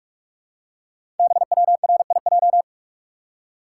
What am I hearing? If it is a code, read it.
Morse code, heard as a keyed tone, about 44 wpm, 700 Hz. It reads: BWRIJ
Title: Code Groups - 5 Letters - 45wpm